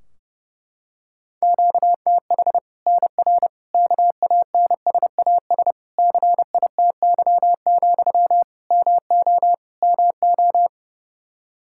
Morse code, heard as a keyed tone, about 32 wpm, 700 Hz. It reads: QTH NR KANSAS CITY, MO MO
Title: Practice QSO - Exchange 1 - Beginning - 30wpm